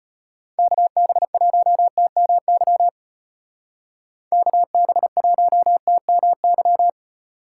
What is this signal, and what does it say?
Morse code, heard as a keyed tone, about 33 wpm, 700 Hz. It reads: KB1TMY KB1TMY